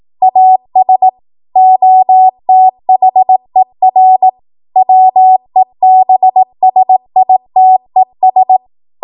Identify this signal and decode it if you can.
Morse code, heard as a keyed tone, about 18 wpm, 750 Hz. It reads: AS OTHER WEBSITES
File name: Archive / 18 WPM / 18WPM